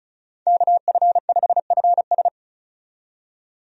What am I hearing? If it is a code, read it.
Morse code, heard as a keyed tone, about 36 wpm, 700 Hz. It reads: KF5FS
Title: USA Call Signs - 5 Characters - 35wpm